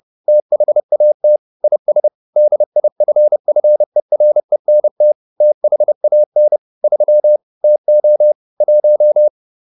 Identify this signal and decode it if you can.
Morse code, heard as a keyed tone, about 30 wpm, 600 Hz. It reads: THAT IS DIFFERENT THAN 3 TO 1